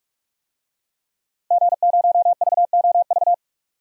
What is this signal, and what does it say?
Morse code, heard as a keyed tone, about 44 wpm, 700 Hz. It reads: G0VOV
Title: DXCC-100 Call Signs - 5 Characters - Repeated 2x - 45wpm